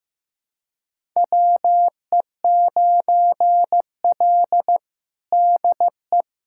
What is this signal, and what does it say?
Morse code, heard as a keyed tone, about 15 wpm, 700 Hz. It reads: WE9L DE
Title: USA Callsign DE Callsign - 4 Characters - 15wpm